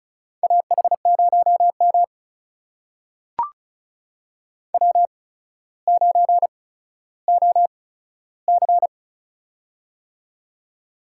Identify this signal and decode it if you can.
Morse code, heard as a keyed tone, about 36 wpm, spaced as for 15 wpm, 700 Hz. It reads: AH0M W9OC
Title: USA Call Signs - 4 Characters - 8x Character Spacing - 35wpm